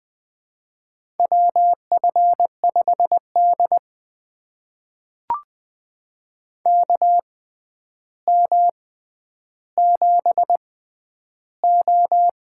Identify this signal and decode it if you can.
Morse code, heard as a keyed tone, about 20 wpm, spaced as for 10 wpm, 700 Hz. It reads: WF5D KM7O
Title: USA Call Signs - 4 Characters - 6x Character Spacing - 20wpm